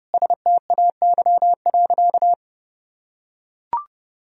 Morse code, heard as a keyed tone, about 32 wpm, 700 Hz. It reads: STAY.
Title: Sentences from Top 1000 Words - Review Part 3 - 30wpm